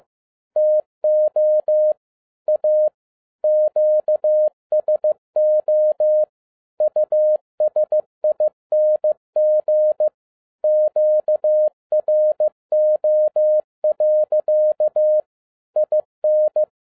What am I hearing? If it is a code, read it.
Morse code, heard as a keyed tone, about 15 wpm, 610 Hz. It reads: TO A QSO USING QRO. IN
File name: 2011-07-mp3 / n22 QRP OTAKU 15wpm